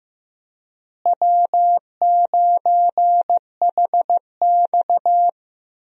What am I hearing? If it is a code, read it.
Morse code, heard as a keyed tone, about 15 wpm, 700 Hz. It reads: W9HX
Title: USA Call Signs - 4 Characters - Repeated 2x - 15wpm